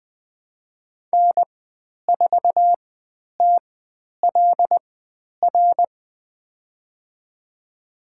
Morse code, heard as a keyed tone, about 20 wpm, spaced as for 10 wpm, 700 Hz. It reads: N4TLR
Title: 102 Lesson 17g - Letter u - Call Signs 20wpm